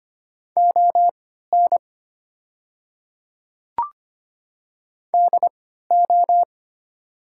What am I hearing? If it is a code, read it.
Morse code, heard as a keyed tone, about 25 wpm, spaced as for 14 wpm, 700 Hz. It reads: ON DO
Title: 3x Character Spacing - Top 200 Words - 25wpm